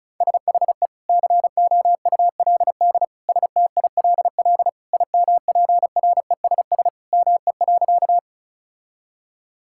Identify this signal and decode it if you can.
Morse code, heard as a keyed tone, about 36 wpm, 700 Hz. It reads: SHE COULD STILL IMPRESS ME.
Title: Sentences from Top 1600 Words 35wpm